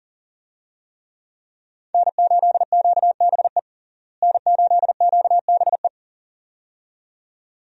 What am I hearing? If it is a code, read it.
Morse code, heard as a keyed tone, about 40 wpm, 700 Hz. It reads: N8QBE N8QBE
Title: USA Call Signs - 5 Characters - Repeated 2x - 40wpm